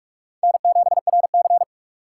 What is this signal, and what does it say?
Morse code, heard as a keyed tone, about 44 wpm, 690 Hz. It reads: N7RC
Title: USA Call Signs - 4 Characters - 4x Character Spacing - 45wpm